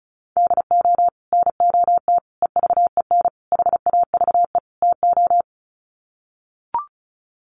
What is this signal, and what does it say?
Morse code, heard as a keyed tone, about 34 wpm, 700 Hz. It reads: DO NOT EVEN HAVE TO
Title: Sets of 5 Words Encore2 35wpm